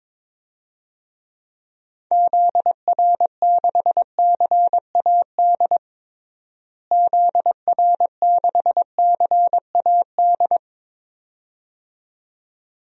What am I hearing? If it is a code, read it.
Morse code, heard as a keyed tone, about 22 wpm, 700 Hz. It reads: ZR6CAD ZR6CAD
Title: DXCC-100 Call Signs - 6 Characters - Repeated 2x - 22wpm